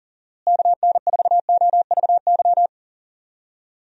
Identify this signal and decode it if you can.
Morse code, heard as a keyed tone, about 40 wpm, 700 Hz. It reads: KN4OVY
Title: USA Call Signs - 6 Characters - Repeated 3x - 40wpm